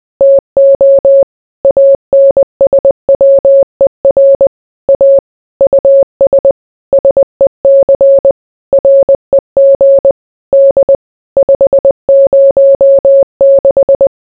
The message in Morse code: TO ANSWER A US SEC REG D 506

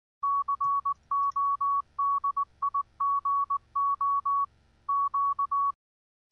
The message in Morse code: CODIGO Q

